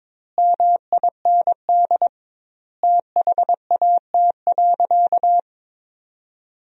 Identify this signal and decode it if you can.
Morse code, heard as a keyed tone, about 22 wpm, 700 Hz. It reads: MIND THAT.
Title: 2x Word-Spacing - Sentences from Top 500 Words - 22wpm